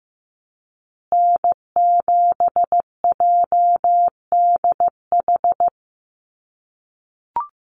Morse code, heard as a keyed tone, about 15 wpm, 700 Hz. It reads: N7JDH
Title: USA Call Signs Repeated 2x 15wpm